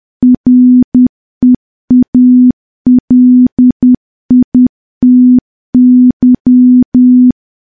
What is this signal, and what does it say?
Morse code, heard as a keyed tone, about 10 wpm, 250 Hz. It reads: REALITY
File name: morse / morse-training-10WPM